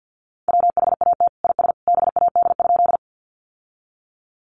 Morse code, heard as a keyed tone, about 50 wpm, 700 Hz. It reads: WHAT IS BAD?
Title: Sentences from Top 700 Words 50wpm